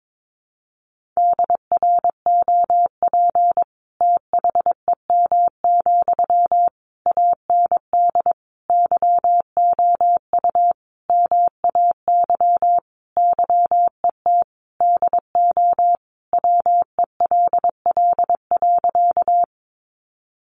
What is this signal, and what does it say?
Morse code, heard as a keyed tone, about 22 wpm, 700 Hz. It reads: DROP THEM, AND YOU MAY YET DO WELL.
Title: Sentences from Top 1400 Words 22wpm